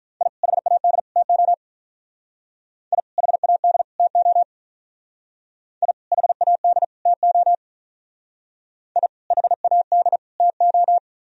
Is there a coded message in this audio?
Yes: Morse code from 0.2 to 11.0 s, heard as a keyed tone, about 45 wpm, 700 Hz. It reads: I HAD TO I HAD TO I HAD TO I HAD TO